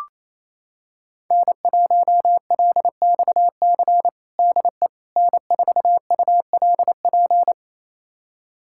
Morse code, heard as a keyed tone, about 27 wpm, 700 Hz. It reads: N1LXC DE N4ULP
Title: USA Callsign DE Callsign - 5 Characters - 28wpm